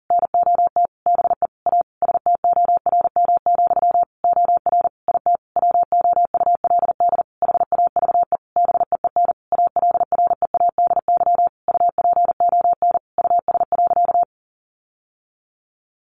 Morse code, heard as a keyed tone, about 40 wpm, 700 Hz. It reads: NOT BE A STORM, OR IT WOULD HAVE BEEN ALREADY UPON US.
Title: Sentences from Top 1500 Words 40wpm